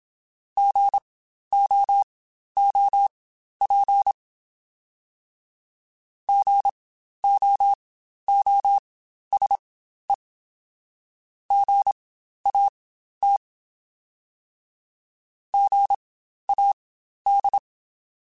Morse code, heard as a keyed tone, about 27 wpm, spaced as for 12 wpm, 800 Hz. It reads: GOOP GOOSE GAT GAD
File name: Spaced Alphas / Spaced Alphas mp3s / Gs sp @25